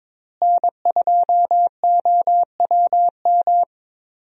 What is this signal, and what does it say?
Morse code, heard as a keyed tone, about 21 wpm, 700 Hz. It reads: N2OWM
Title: USA Call Signs - 5 Characters - Repeated 3x - 22wpm